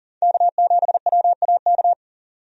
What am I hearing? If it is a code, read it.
Morse code, heard as a keyed tone, about 40 wpm, 700 Hz. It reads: K7WAK
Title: USA Call Signs - 5 Characters - 40wpm